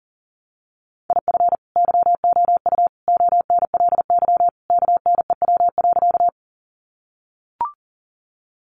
Morse code, heard as a keyed tone, about 40 wpm, 700 Hz. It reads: IF YOU ONLY KNEW.
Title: Sentences from Top 400 Words 40wpm